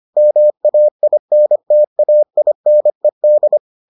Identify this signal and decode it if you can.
Morse code, heard as a keyed tone, about 25 wpm, 600 Hz. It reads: MAINTAINED